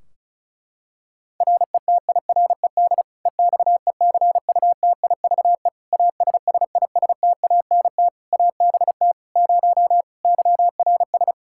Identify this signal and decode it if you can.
Morse code, heard as a keyed tone, about 34 wpm, 700 Hz. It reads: RETIRED EXECUTIVE ASSISTANT ABT 0 YRS
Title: K1USN SST QSO Practice 35wpm